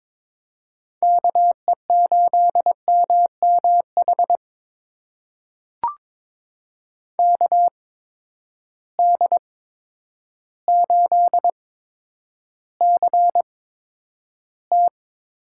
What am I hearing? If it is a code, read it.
Morse code, heard as a keyed tone, about 23 wpm, spaced as for 10 wpm, 700 Hz. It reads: KE8MMH KD8CT
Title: USA Call Signs - 6 Characters - 8x Character Spacing - 22wpm